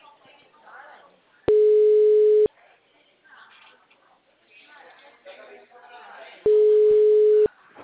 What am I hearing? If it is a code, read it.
Ringback tone (Europe, 425 Hz), with speech.
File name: out- -2022- - - .36